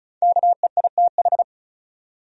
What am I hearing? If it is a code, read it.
Morse code, heard as a keyed tone, about 36 wpm, 700 Hz. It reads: KEITH
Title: USA Common Names - Max 5 Letters - 35wpm